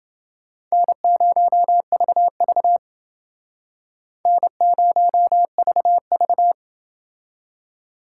Morse code, heard as a keyed tone, about 28 wpm, 700 Hz. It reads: N0VV N0VV